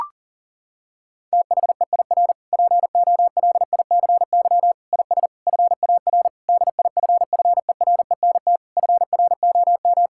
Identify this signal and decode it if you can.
Morse code, heard as a keyed tone, about 40 wpm, 700 Hz. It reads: THEIR POLICY IS FAR DIFFERENT FROM
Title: Sentences from Top 1000 Words - Review Part 5 - 40wpm